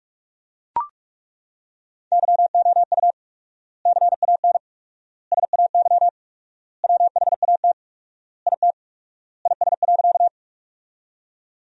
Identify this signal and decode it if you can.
Morse code, heard as a keyed tone, about 42 wpm, 700 Hz. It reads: YOU CAN SAY WHAT IT IS.
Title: 4x Word-Spacing - Sentences from Top 200 Words - 45wpm